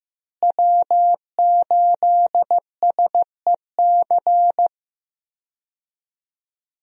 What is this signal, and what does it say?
Morse code, heard as a keyed tone, about 14 wpm, 700 Hz. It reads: W8SEC
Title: USA Call Signs - 5 Characters - Repeated 2x - 15wpm